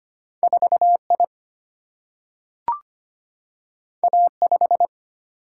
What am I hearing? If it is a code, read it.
Morse code, heard as a keyed tone, about 25 wpm, 700 Hz. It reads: 4 I A 5
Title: Single-Letter Single-Number 25wpm